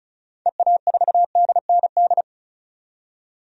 Morse code, heard as a keyed tone, about 34 wpm, 700 Hz. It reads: EA4DND
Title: DXCC-100 Call Signs - 6 Characters - 35wpm